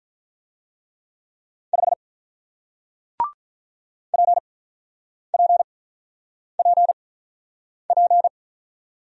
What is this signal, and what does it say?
Morse code, heard as a keyed tone, about 45 wpm, spaced as for 9 wpm, 700 Hz. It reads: 5 PPPP